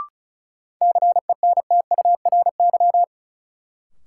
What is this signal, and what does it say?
Morse code, heard as a keyed tone, about 36 wpm, 700 Hz. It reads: CENTURY
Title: Sets of 5 Words - New Word Drill - 35wpm